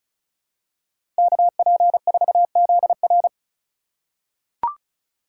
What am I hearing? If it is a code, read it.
Morse code, heard as a keyed tone, about 34 wpm, 690 Hz. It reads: KP4ZR
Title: USA Call Signs - 5 Characters - 2x Character Spacing - 35wpm